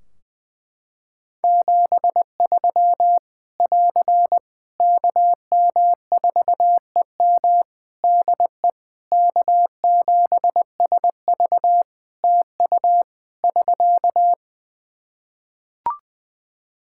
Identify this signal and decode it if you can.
Morse code, heard as a keyed tone, about 20 wpm, 700 Hz. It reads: 73 <AR> KM4EM DE K7SV TU <SK>
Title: Practice QSO - Exchange 4 - Ending - 20wpm